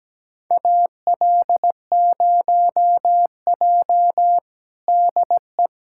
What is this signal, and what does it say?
Morse code, heard as a keyed tone, about 17 wpm, 700 Hz. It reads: AL0J DE
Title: USA Callsign DE Callsign - 4 Characters - 17wpm